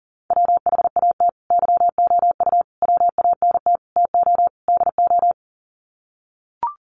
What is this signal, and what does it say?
Morse code, heard as a keyed tone, about 40 wpm, 700 Hz. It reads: WHAT YOU WANT TO DO
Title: Sets of 5 Words Encore 40wpm